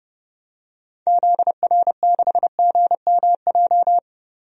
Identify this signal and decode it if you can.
Morse code, heard as a keyed tone, about 30 wpm, 700 Hz. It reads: ZR6GMJ
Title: DXCC-100 Call Signs - 6 Characters - Repeated 3x - 30wpm